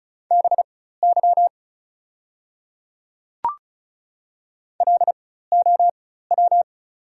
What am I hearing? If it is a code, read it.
Morse code, heard as a keyed tone, about 34 wpm, spaced as for 16 wpm, 690 Hz. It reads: BY LOW
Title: 4x Character Spacing - Top 200 Words - 35wpm